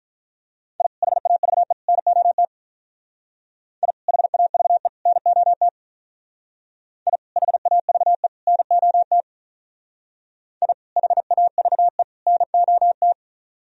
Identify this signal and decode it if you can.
Morse code, heard as a keyed tone, about 43 wpm, 700 Hz. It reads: I HAVE NOT I HAVE NOT I HAVE NOT I HAVE NOT
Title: Speed-Racing - Sets of 3 Words 35wpm